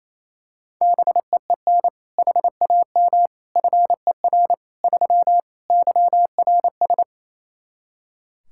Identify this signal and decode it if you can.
Morse code, heard as a keyed tone, about 28 wpm, 700 Hz. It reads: BEEN HAM FER 3 YRS
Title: K1USN SST QSO Practice 28wpm